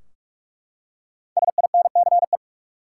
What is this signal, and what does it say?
Morse code, heard as a keyed tone, about 41 wpm, 700 Hz. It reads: SINCE
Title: Five Letter Words - 45wpm